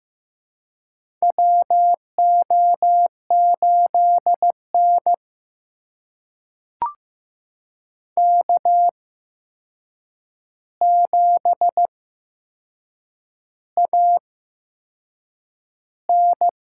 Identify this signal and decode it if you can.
Morse code, heard as a keyed tone, about 15 wpm, spaced as for 6 wpm, 700 Hz. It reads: WO8N K7AN
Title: USA Call Signs - 4 Characters - 8x Character Spacing - 15wpm